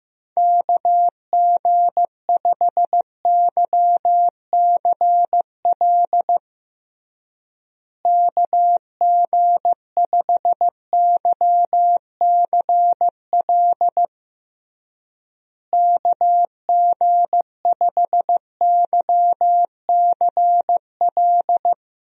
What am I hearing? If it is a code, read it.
Morse code, heard as a keyed tone, about 16 wpm, 700 Hz. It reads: KG5YCL KG5YCL KG5YCL
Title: USA Call Signs - 6 Characters - Repeated 3x - 15wpm